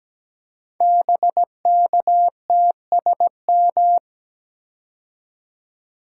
Morse code, heard as a keyed tone, about 17 wpm, 700 Hz. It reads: BKTSM
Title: Code Groups - 5 Letters - 17wpm